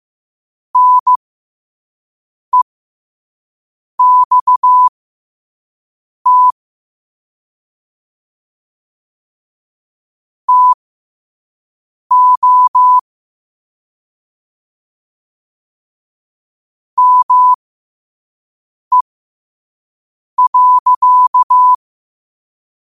Morse code, heard as a keyed tone, about 15 wpm, spaced as for 5 wpm, 1000 Hz. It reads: NEXT TO ME.
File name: qotdfiles / 5wpm